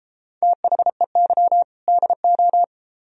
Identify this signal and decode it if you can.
Morse code, heard as a keyed tone, about 31 wpm, 700 Hz. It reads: THEY DO